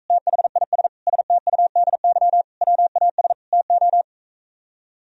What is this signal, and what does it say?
Morse code, heard as a keyed tone, about 45 wpm, 700 Hz. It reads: THIS STUDY WAS TO